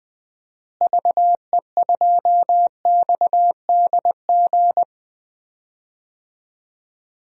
Morse code, heard as a keyed tone, about 20 wpm, 700 Hz. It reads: VE2XDG
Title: DXCC-100 Call Signs - 6 Characters - Repeated 3x - 20wpm